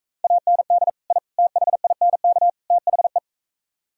Morse code, heard as a keyed tone, about 42 wpm, 700 Hz. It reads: AND I THINK THE